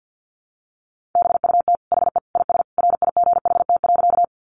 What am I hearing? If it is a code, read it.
Morse code, heard as a keyed tone, about 50 wpm, 700 Hz. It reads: BUT HE IS RIGHT.
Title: Sentences from Top 200 Words 50wpm